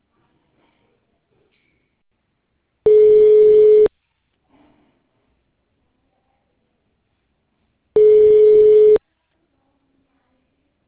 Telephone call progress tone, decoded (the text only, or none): ringback tone